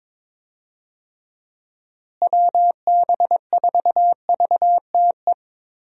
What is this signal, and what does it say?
Morse code, heard as a keyed tone, about 22 wpm, 700 Hz. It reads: WB4VTE
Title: USA Call Signs - 6 Characters - Repeated 3x - 22wpm